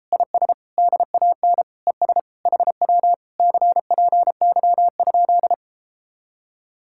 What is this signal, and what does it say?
Morse code, heard as a keyed tone, about 33 wpm, 700 Hz. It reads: IS DAN ES HW CPY?